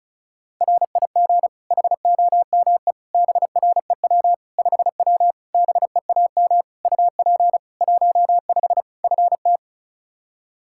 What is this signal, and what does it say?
Morse code, heard as a keyed tone, about 36 wpm, 700 Hz. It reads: RIG HOME BREW 5W BEAM UP 15 FT
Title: K1USN SST QSO Practice 35wpm